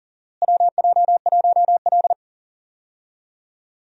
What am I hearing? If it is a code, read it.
Morse code, heard as a keyed tone, about 40 wpm, 700 Hz. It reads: WJ1L
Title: USA Call Signs - 4 Characters - Repeated 2x - 40wpm